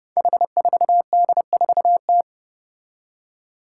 Morse code, heard as a keyed tone, about 28 wpm, 700 Hz. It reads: H4D4T
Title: Five Challenging Letter-Number Combinations 30wpm